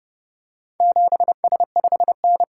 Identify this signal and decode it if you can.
Morse code, heard as a keyed tone, about 29 wpm, 700 Hz. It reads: 7S5N